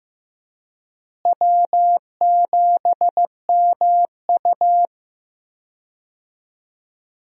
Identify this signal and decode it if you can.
Morse code, heard as a keyed tone, about 15 wpm, 700 Hz. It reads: W7MU
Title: USA Call Signs - 4 Characters - Repeated 2x - 15wpm